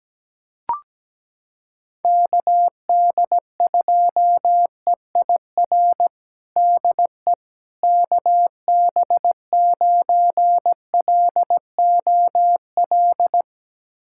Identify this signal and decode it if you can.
Morse code, heard as a keyed tone, about 17 wpm, 700 Hz. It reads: KD2EIR DE KB9LOL
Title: USA Callsign DE Callsign - 6 Characters - 17wpm